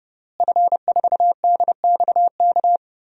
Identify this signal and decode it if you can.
Morse code, heard as a keyed tone, about 30 wpm, 700 Hz. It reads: F4DXK
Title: DXCC-100 Call Signs - 5 Characters - Repeated 2x - 30wpm